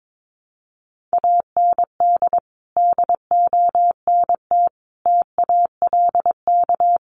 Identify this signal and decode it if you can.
Morse code, heard as a keyed tone, about 22 wpm, 700 Hz. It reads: AND DONT TALK
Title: Sentences from Top 400 Words 22wpm